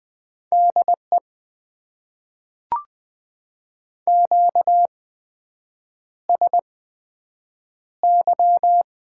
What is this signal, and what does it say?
Morse code, heard as a keyed tone, about 20 wpm, spaced as for 7 wpm, 700 Hz. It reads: DE QSY
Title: Top 90 QSO Elements - 8x Character Spacing - 20wpm